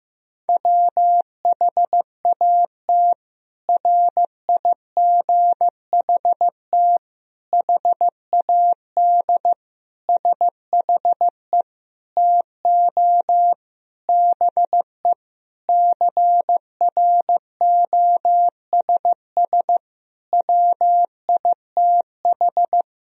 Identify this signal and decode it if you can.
Morse code, heard as a keyed tone, about 15 wpm, 700 Hz. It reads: WHAT RIGHT HAD SHE TO BE CROSS WITH
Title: Sentences from Top 1000 Words - Review Part 1 - 15wpm